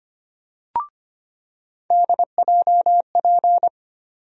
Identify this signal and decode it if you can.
Morse code, heard as a keyed tone, about 25 wpm, 700 Hz. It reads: DJP
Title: USA Call Sign Postfixes - 3 Letters 25wpm